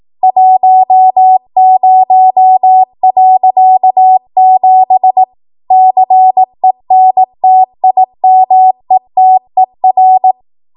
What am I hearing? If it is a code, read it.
Morse code, heard as a keyed tone, about 18 wpm, 750 Hz. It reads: 10.7 CENTIMETER
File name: Morse / 18WPM